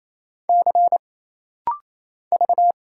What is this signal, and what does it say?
Morse code, heard as a keyed tone, about 27 wpm, 700 Hz. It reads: C V
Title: Single Letter-Number - Rapid-Fire 28wpm